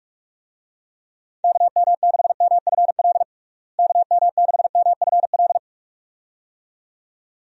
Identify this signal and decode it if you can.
Morse code, heard as a keyed tone, about 46 wpm, 700 Hz. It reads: KM6MFL KM6MFL
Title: USA Call Signs - 6 Characters - Repeated 2x - 45wpm